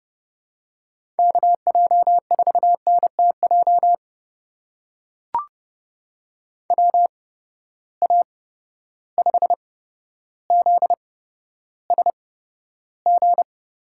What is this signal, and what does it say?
Morse code, heard as a keyed tone, about 31 wpm, spaced as for 13 wpm, 700 Hz. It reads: KJ4NTJ WA5ZSG
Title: USA Call Signs - 6 Characters - 8x Character Spacing - 30wpm